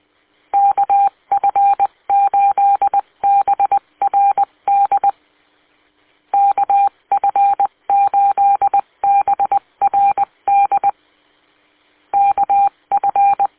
Morse code, heard as a keyed tone, about 20 wpm, 800 Hz. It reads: KF8BRD KF8BRD KF
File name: w8cmn- -combined